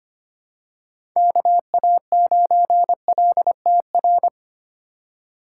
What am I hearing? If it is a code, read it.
Morse code, heard as a keyed tone, about 25 wpm, 700 Hz. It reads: KA9LTR
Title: USA Call Signs - 6 Characters - 4x Character Spacing - 25wpm